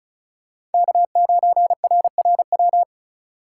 Morse code, heard as a keyed tone, about 36 wpm, 700 Hz. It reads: K9RRW
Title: USA Call Signs - 5 Characters - 6x Character Spacing - 35wpm